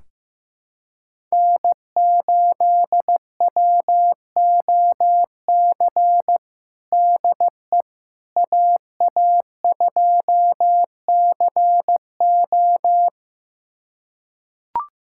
Morse code, heard as a keyed tone, about 15 wpm, 700 Hz. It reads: N8WOC DE AA2CO
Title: USA Callsign DE Callsign - 5 Characters - 15wpm